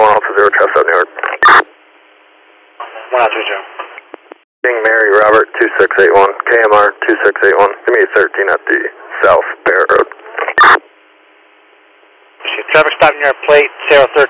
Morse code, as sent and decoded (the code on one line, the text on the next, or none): none